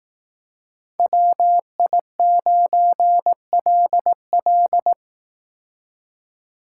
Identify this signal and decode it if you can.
Morse code, heard as a keyed tone, about 18 wpm, 700 Hz. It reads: WI9LL
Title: Speed Racing - USA Call Signs 15wpm